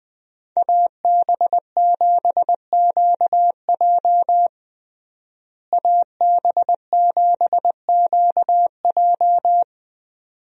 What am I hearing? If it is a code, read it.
Morse code, heard as a keyed tone, about 21 wpm, 700 Hz. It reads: AB7QJ AB7QJ
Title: USA Call Signs - 5 Characters - Repeated 2x - 20wpm